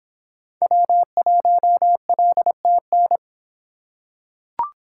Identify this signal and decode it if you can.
Morse code, heard as a keyed tone, about 24 wpm, 700 Hz. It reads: W1LTN